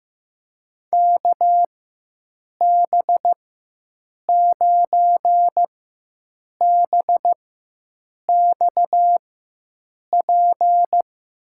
Morse code, heard as a keyed tone, about 15 wpm, spaced as for 7 wpm, 700 Hz. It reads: KB9BXP
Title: USA Call Signs - 6 Characters - 4x Character Spacing - 15wpm